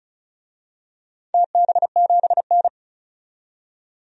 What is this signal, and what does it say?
Morse code, heard as a keyed tone, about 36 wpm, 700 Hz. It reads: TB7N
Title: DXCC-100 Call Signs - 4 Characters - 35wpm